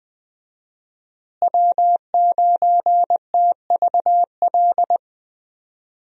Morse code, heard as a keyed tone, about 20 wpm, 700 Hz. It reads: W9TVL
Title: USA Call Signs - 5 Characters - Repeated 2x - 20wpm